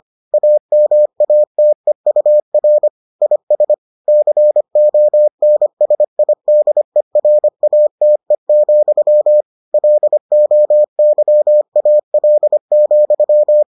Morse code, heard as a keyed tone, about 25 wpm, 600 Hz. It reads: AMATEUR IS CONSIDERATE, LOYAL,